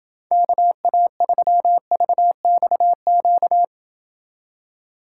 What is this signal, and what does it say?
Morse code, heard as a keyed tone, about 29 wpm, 700 Hz. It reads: KA3VXQ